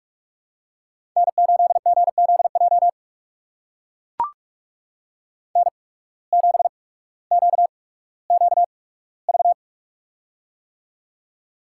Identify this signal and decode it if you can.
Morse code, heard as a keyed tone, about 44 wpm, spaced as for 19 wpm, 700 Hz. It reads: N8GZJ N7QQV
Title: USA Call Signs - 5 Characters - 8x Character Spacing - 45wpm